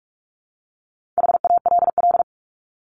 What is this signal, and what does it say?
Morse code, heard as a keyed tone, about 44 wpm, 700 Hz. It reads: HALL
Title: Top 1751-2000 Words 45wpm